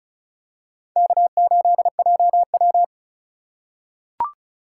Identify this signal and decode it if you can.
Morse code, heard as a keyed tone, about 34 wpm, 700 Hz. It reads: K8JW
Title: USA Call Signs - 4 Characters - 2x Character Spacing - 35wpm